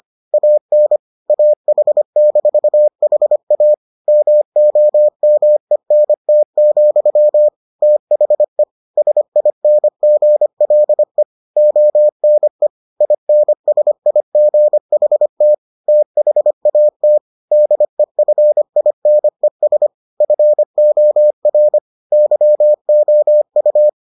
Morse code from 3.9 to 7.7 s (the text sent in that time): MOMENT,